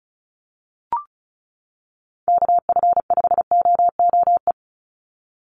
Morse code, heard as a keyed tone, about 34 wpm, 700 Hz. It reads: KF5OOE
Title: USA Call Signs 35wpm